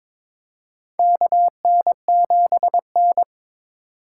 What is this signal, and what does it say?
Morse code, heard as a keyed tone, about 22 wpm, 700 Hz. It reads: KN7N
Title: USA Call Signs - 4 Characters - 3x Character Spacing - 22wpm